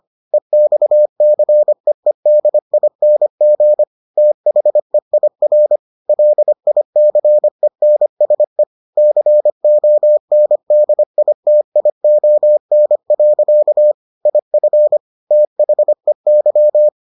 Morse code, heard as a keyed tone, about 25 wpm, 600 Hz. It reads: EXCEEDING THEIR LICENSE CONDITION. IF THEY